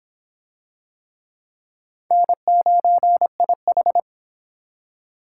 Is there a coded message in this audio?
Yes: Morse code from 2.1 to 4.0 s, heard as a keyed tone, about 25 wpm, 700 Hz. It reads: N9IH